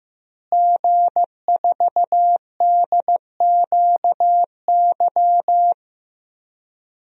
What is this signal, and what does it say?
Morse code, heard as a keyed tone, about 15 wpm, 700 Hz. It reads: G4DQY
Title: DXCC-100 Call Signs - 5 Characters - Repeated 2x - 15wpm